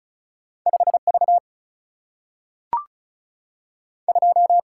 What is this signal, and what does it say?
Morse code, heard as a keyed tone, about 36 wpm, 700 Hz. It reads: 5 V 2
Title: Single-Letter Single-Number 35wpm